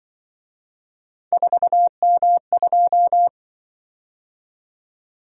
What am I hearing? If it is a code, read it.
Morse code, heard as a keyed tone, about 24 wpm, 700 Hz. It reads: 4M2
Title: Speed-Racing - Three Letter-Number Combinations - 20wpm